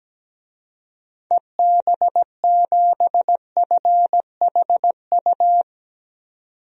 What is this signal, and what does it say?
Morse code, heard as a keyed tone, about 17 wpm, 700 Hz. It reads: EB7FHU
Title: DXCC-100 Call Signs - 6 Characters - 17wpm